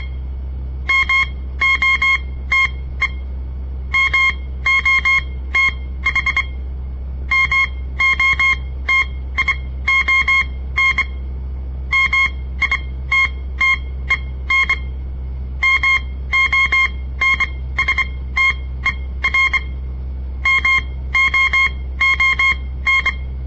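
Morse code, heard as a keyed tone, about 25 wpm, spaced as for 16 wpm, 2150 Hz. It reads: MOTE MOTH MOTION MITTEN MONSTER MOON
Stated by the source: On-air Morse code practice broadcast by a southern Montana amateur radio association; all code, no read back.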